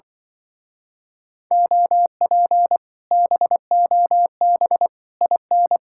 Morse code, heard as a keyed tone, about 24 wpm, 700 Hz. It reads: OP BOB IN
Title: Speed-Racing Common QSO Phrases Encore - 20wpm